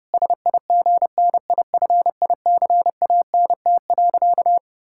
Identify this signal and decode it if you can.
Morse code, heard as a keyed tone, about 28 wpm, 700 Hz. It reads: SIGNIFICANT.
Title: Sentences from Top 1000 Words - Review Part 6 - 30wpm